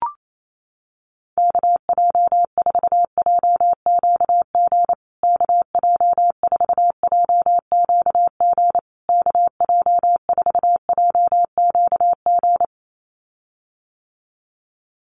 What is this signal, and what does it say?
Morse code, heard as a keyed tone, about 28 wpm, 700 Hz. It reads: KJ4JQG KJ4JQG KJ4JQG
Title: USA Call Signs Repeated 3x 28wpm